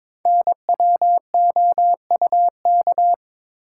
Morse code, heard as a keyed tone, about 23 wpm, 690 Hz. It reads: NWOUK
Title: Code Groups - 5 Letters - 22wpm